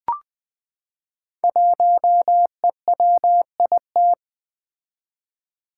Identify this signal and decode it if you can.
Morse code, heard as a keyed tone, about 20 wpm, 700 Hz. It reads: 1EWIT
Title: Five Challenging Letter-Number Combinations 20wpm